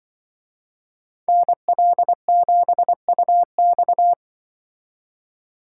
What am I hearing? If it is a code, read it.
Morse code, heard as a keyed tone, about 24 wpm, 700 Hz. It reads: NL7UX
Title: Speed Racing - USA 2x2 Call Signs 20wpm